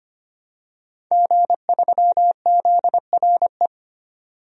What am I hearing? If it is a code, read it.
Morse code, heard as a keyed tone, about 25 wpm, 700 Hz. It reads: G3ZRE
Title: DXCC-100 Call Signs - 5 Characters - 25wpm